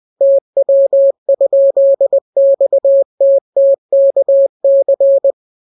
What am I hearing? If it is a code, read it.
Morse code, heard as a keyed tone, about 20 wpm, 550 Hz. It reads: TW?XTTKC